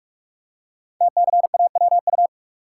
Morse code, heard as a keyed tone, about 44 wpm, 700 Hz. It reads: TCAWU
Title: Code Groups - 5 Letter:Numbers - 45wpm